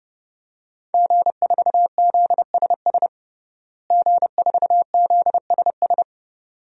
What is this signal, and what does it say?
Morse code, heard as a keyed tone, about 30 wpm, 700 Hz. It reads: G4ZSS G4ZSS
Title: DXCC-100 Call Signs - 5 Characters - Repeated 2x - 30wpm